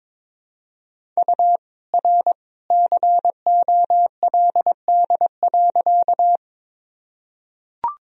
Morse code, heard as a keyed tone, about 22 wpm, 700 Hz. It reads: U R COLD.
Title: Sentences from Top 400 Words and Abbreviations 22wpm